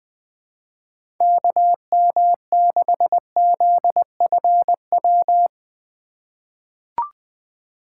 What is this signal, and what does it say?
Morse code, heard as a keyed tone, about 20 wpm, 700 Hz. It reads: KM6ZFW
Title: USA Call Signs - 6 Characters - 4x Character Spacing - 20wpm